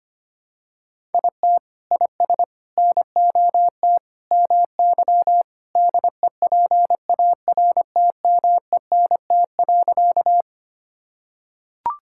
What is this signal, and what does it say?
Morse code, heard as a keyed tone, about 25 wpm, 700 Hz. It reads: IT IS NOT MY DEPARTMENT.
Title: Sentences from Top 1000 Words - Review Part 5 - 25wpm